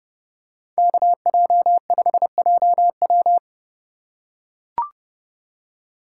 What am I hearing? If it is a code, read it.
Morse code, heard as a keyed tone, about 29 wpm, 700 Hz. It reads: KJ5JW